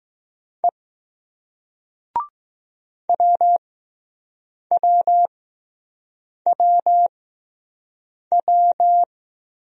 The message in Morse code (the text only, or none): E WWWW